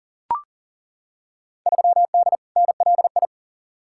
Morse code, heard as a keyed tone, about 40 wpm, 700 Hz. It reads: 3D NLI
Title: Field Day Exchanges Encore - 40wpm